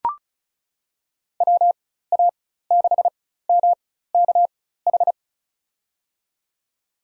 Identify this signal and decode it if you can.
Morse code, heard as a keyed tone, about 36 wpm, spaced as for 16 wpm, 700 Hz. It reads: WA6MKH
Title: USA Call Signs - 6 Characters - 4x Character Spacing - 35wpm